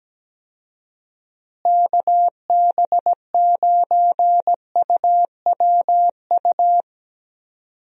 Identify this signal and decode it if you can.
Morse code, heard as a keyed tone, about 17 wpm, 700 Hz. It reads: KB9UWU
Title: USA Call Signs - 6 Characters - Repeated 2x - 17wpm